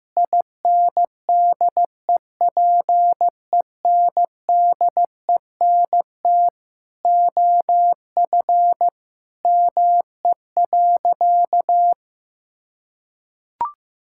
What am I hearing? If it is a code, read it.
Morse code, heard as a keyed tone, about 16 wpm, 700 Hz. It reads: INDEPENDENT OF ME.
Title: Sentences from Top 1000 Words - Review Part 6 - 15wpm